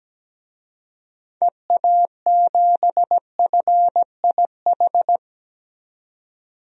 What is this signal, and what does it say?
Morse code, heard as a keyed tone, about 17 wpm, 700 Hz. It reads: EA7FIH
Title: DXCC-100 Call Signs - 6 Characters - 17wpm